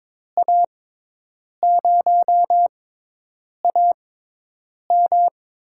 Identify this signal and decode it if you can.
Morse code, heard as a keyed tone, about 21 wpm, spaced as for 8 wpm, 700 Hz. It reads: A0AM
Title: USA Call Signs - 5 Characters - 6x Character Spacing - 22wpm